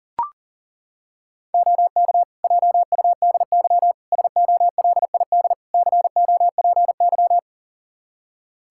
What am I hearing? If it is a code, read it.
Morse code, heard as a keyed tone, about 40 wpm, 700 Hz. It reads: OK JUDY SOLID COPY
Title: Practice QSO - Exchange 3 - More Info - 40wpm